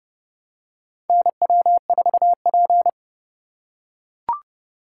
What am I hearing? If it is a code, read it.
Morse code, heard as a keyed tone, about 31 wpm, 700 Hz. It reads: NW4P
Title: USA Call Signs - 4 Characters - 6x Character Spacing - 30wpm